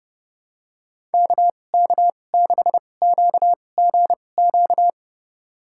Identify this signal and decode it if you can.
Morse code, heard as a keyed tone, about 30 wpm, spaced as for 22 wpm, 700 Hz. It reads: KK6QGQ
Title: USA Call Signs - 6 Characters - 2x Character Spacing - 30wpm